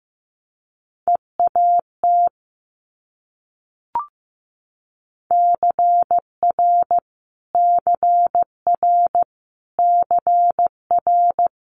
Morse code, heard as a keyed tone, about 15 wpm, 700 Hz. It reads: EAT CR CR CR
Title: Multi-letter Phonemes Repeated 3x 15wpm